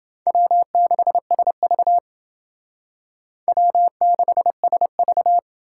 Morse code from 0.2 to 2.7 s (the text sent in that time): W6SV